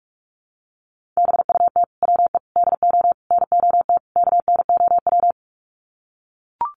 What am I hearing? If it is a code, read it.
Morse code, heard as a keyed tone, about 44 wpm, 700 Hz. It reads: BUT WE DO NOT KNOW
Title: Sets of 5 Words Encore2 45wpm